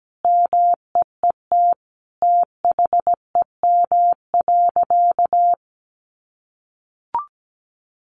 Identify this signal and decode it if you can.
Morse code, heard as a keyed tone, about 17 wpm, 700 Hz. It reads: MEET THEM.
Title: Sentences from Top 1100 Words 17wpm